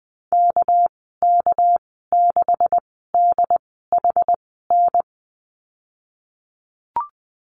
Morse code, heard as a keyed tone, about 19 wpm, spaced as for 14 wpm, 700 Hz. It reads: KK6DHN
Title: 219 Lesson 35g - Number 0 - Call Signs 20wpm